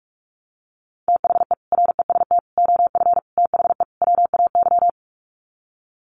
Morse code, heard as a keyed tone, about 44 wpm, 700 Hz. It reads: THE REST OF THE WAY
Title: Sets of 5 Words Encore 45wpm